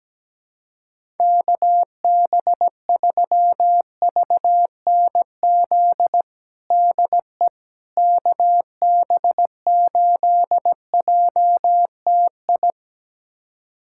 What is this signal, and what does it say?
Morse code, heard as a keyed tone, about 17 wpm, 700 Hz. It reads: KB3VNZ DE KB8JTI
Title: USA Callsign DE Callsign 17wpm